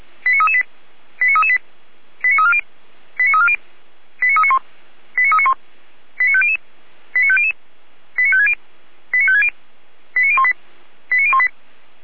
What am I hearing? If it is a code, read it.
Five-tone selective call (ZVEI), heard as quick runs of five tones, tones of about 70 ms. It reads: 89338 89338 89340 89340 89391 89391 89500 89500 89560 89560 89018 89018